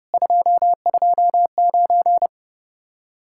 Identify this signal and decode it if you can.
Morse code, heard as a keyed tone, about 34 wpm, 700 Hz. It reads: 229